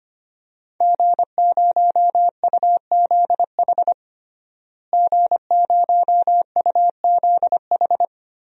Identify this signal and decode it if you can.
Morse code, heard as a keyed tone, about 25 wpm, 700 Hz. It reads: G0UZH G0UZH
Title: DXCC-100 Call Signs - 5 Characters - Repeated 3x - 25wpm